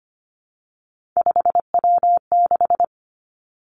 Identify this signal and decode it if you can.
Morse code, heard as a keyed tone, about 25 wpm, 700 Hz. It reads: 5W6
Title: Three Challenging Letter-Number Combinations 25wpm